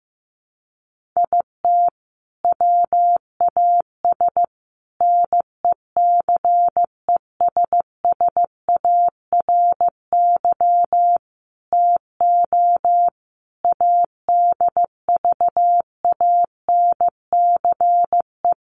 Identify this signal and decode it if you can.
Morse code, heard as a keyed tone, about 15 wpm, 700 Hz. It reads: IT WAS NECESSARY TO ADVANCE
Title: Sentences from Top 1400 Words 15wpm